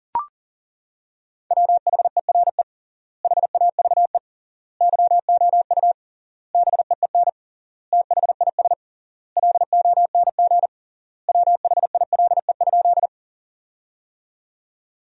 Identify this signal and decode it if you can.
Morse code, heard as a keyed tone, about 40 wpm, 700 Hz. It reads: WHERE HAVE YOU BEEN THIS LONG WHILE?
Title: 3x Word-Spacing - Sentences from Top 300 Words - 40wpm